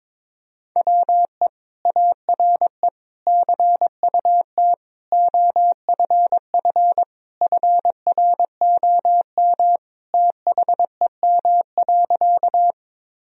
Morse code, heard as a keyed tone, about 21 wpm, 700 Hz. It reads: WE ARE CUT OFF FROM THEM.
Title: Sentences from Top 1000 Words - Review Part 1 - 22wpm